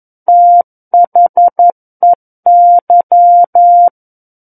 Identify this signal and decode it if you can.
Morse code, heard as a keyed tone, about 11 wpm, 700 Hz. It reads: THEY
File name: 11 WPM